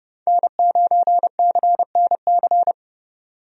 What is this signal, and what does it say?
Morse code, heard as a keyed tone, about 32 wpm, 700 Hz. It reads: N9CNC
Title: USA Call Signs - 5 Characters - 30wpm